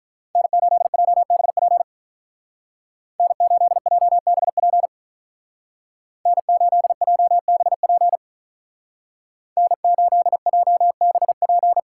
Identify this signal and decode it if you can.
Morse code, heard as a keyed tone, about 44 wpm, 700 Hz. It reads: N8JBP N8JBP N8JBP N8JBP
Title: Speed Racing - USA 1x3 Call Signs 35wpm